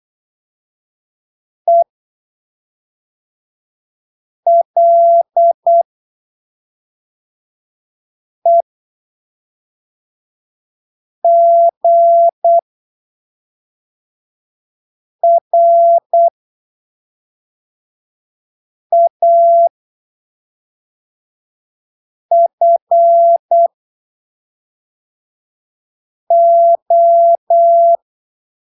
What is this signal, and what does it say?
Morse code, heard as a keyed tone, about 8 wpm, spaced as for 3 wpm, 680 Hz. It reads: ELEGRAFO